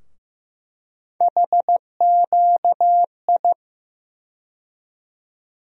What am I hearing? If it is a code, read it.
Morse code, heard as a keyed tone, about 15 wpm, 700 Hz. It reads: HQI
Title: USA Call Sign Postfixes - 3 Letters 15wpm